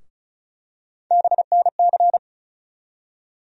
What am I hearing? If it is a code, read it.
Morse code, heard as a keyed tone, about 34 wpm, 700 Hz. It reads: BNC
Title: USA Call Sign Postfixes - 3 Letters 35wpm